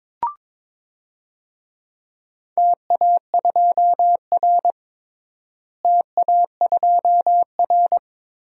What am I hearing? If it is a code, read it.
Morse code, heard as a keyed tone, about 21 wpm, 700 Hz. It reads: TA2R TA2R
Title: DXCC-100 Call Signs - 4 Characters - Repeated 2x - 22wpm